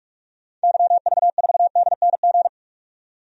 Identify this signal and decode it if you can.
Morse code, heard as a keyed tone, about 45 wpm, 700 Hz. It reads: YV4DNG
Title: DXCC-100 Call Signs - 6 Characters - 45wpm